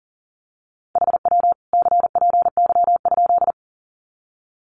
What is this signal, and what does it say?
Morse code, heard as a keyed tone, about 40 wpm, 700 Hz. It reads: HW CPY?
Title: Common QSO Phrases 40wpm